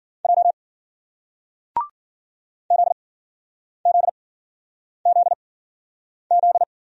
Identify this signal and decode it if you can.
Morse code, heard as a keyed tone, about 52 wpm, spaced as for 9 wpm, 700 Hz. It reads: J ZZZZ